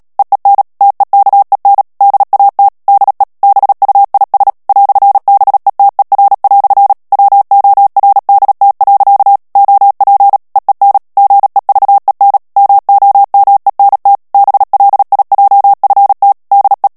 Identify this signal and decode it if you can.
Morse code, heard as a keyed tone, about 37 wpm, 800 Hz. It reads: EEN TEKEN DAT DE BUIS "BETER" WORDT. OP EEN GEVEN MOMENT BLIJFT DE
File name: morse mp3 / HOLLANDAIS texte clair 2 vitesse 35 mots minute